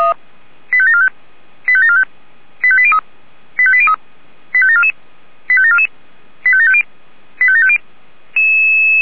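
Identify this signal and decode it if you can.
Five-tone selective call (ZVEI), heard as quick runs of five tones, tones of about 70 ms. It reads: 85745 85745 85803 85803 85744 85744 85750 85750
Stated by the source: ZVEI five-tone call 85730 received on an SDR.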